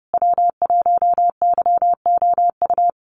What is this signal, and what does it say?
Morse code, heard as a keyed tone, about 30 wpm, 700 Hz. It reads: W1YOU
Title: USA Call Signs Repeated 3x 30wpm